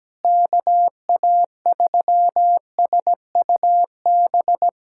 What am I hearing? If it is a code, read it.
Morse code, heard as a keyed tone, about 17 wpm, 700 Hz. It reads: KA3SUB
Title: USA Callsign DE Callsign - 6 Characters - 17wpm